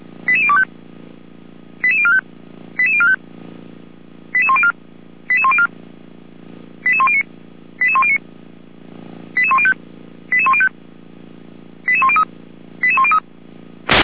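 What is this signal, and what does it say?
Five-tone selective call (ZVEI), heard as quick runs of five tones, tones of about 70 ms. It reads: 88025 88045 88045 88104 88104 88109 88109 88105 88105 88103 88103
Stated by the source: SDR-received ZVEI five-tone call 88102, Feuerwehr Orlamünde.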